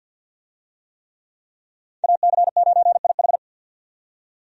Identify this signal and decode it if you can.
Morse code, heard as a keyed tone, about 52 wpm, 700 Hz. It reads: AC9IH
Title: USA Call Signs - 5 Characters - 50wpm